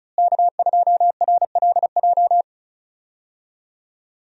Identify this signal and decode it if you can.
Morse code, heard as a keyed tone, about 36 wpm, 700 Hz. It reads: K2RLJ
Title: USA Call Signs - 5 Characters - Repeated 2x - 35wpm